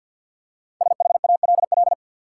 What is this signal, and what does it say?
Morse code, heard as a keyed tone, about 47 wpm, 700 Hz. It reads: SHALL
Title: Five Letter Words - 50wpm